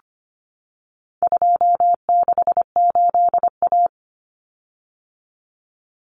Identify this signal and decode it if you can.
Morse code, heard as a keyed tone, about 25 wpm, 700 Hz. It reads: 268A
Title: Common Cut Numbers - 4-Digit 25wpm